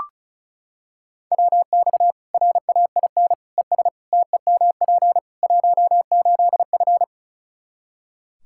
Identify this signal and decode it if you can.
Morse code, heard as a keyed tone, about 34 wpm, 700 Hz. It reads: WX RAIN ES TEMP 18F
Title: K1USN SST QSO Practice 35wpm